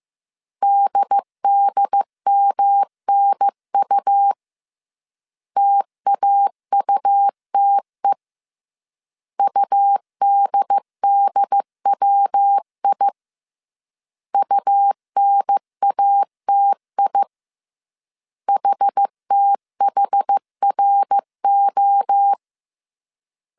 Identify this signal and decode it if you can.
Morse code, heard as a keyed tone, about 15 wpm, 800 Hz. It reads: DDMNU TAUTE UDDWI UNATI HTHRO